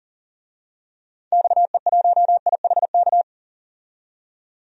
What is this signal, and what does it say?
Morse code, heard as a keyed tone, about 40 wpm, 700 Hz. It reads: XE1IHK
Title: DXCC-100 Call Signs - 6 Characters - 40wpm